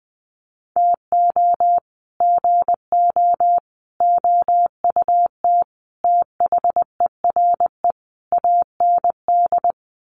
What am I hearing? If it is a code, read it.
Morse code, heard as a keyed tone, about 19 wpm, 700 Hz. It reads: TO GO OUT THERE AND
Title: Sets of 5 Words Encore2 20wpm